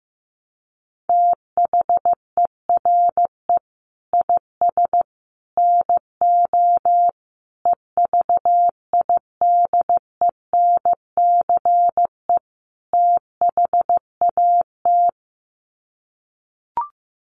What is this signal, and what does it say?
Morse code, heard as a keyed tone, about 15 wpm, 700 Hz. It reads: THERE IS NO EVIDENCE THAT
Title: Sets of 5 Words Encore 15wpm